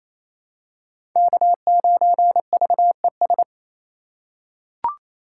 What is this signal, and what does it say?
Morse code, heard as a keyed tone, about 28 wpm, 700 Hz. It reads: K9VES
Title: USA Call Signs - 6x Character Spacing - 28wpm